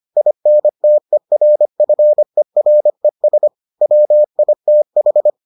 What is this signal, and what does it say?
Morse code, heard as a keyed tone, about 25 wpm, 600 Hz. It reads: INTERFERES WITH